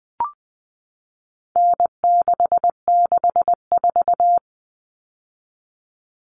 Morse code, heard as a keyed tone, about 20 wpm, 700 Hz. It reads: N664
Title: All Cut Numbers - 4-Digit 20wpm